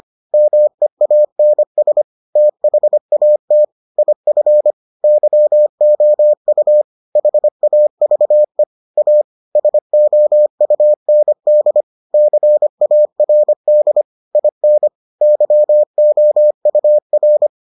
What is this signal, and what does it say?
Morse code, heard as a keyed tone, about 25 wpm, 600 Hz. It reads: MEANS THAT IF YOU HAVE A SOUND CARD IN YOUR